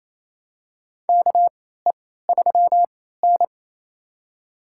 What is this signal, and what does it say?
Morse code, heard as a keyed tone, about 28 wpm, spaced as for 16 wpm, 700 Hz. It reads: KE3N
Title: USA Call Signs - 3x Character Spacing - 28wpm